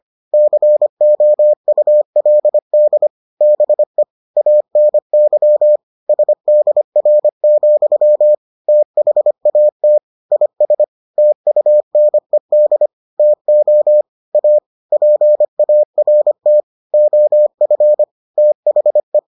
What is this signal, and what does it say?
Morse code, heard as a keyed tone, about 25 wpm, 600 Hz. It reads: COULD BE ANY SDR, THAT IS TUNED TO A PART OF THE